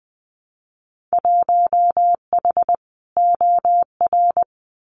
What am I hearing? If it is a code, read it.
Morse code, heard as a keyed tone, about 20 wpm, 700 Hz. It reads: 1H OR
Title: Winter Field Day Exchanges 20wpm